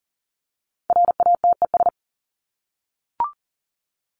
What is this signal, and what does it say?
Morse code, heard as a keyed tone, about 40 wpm, 700 Hz. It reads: RATES
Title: ES:S Suffix - Top 100 Words with Base Words 40wpm